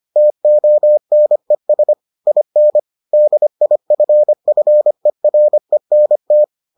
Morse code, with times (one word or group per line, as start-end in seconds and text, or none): 0.2-1.9 TONES
2.3-2.8 IN
3.1-6.4 DIFFERENT